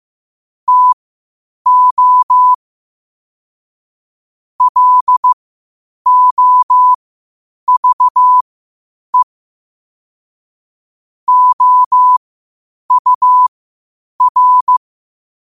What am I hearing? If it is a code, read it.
Morse code, heard as a keyed tone, about 15 wpm, spaced as for 8 wpm, 1000 Hz. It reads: TO LOVE OUR
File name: qotdfiles / 8wpm